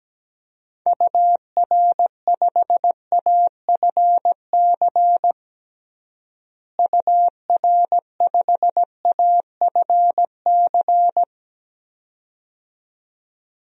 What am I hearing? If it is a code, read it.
Morse code, heard as a keyed tone, about 17 wpm, 700 Hz. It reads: UR5AFC UR5AFC
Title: DXCC-100 Call Signs - 6 Characters - Repeated 2x - 17wpm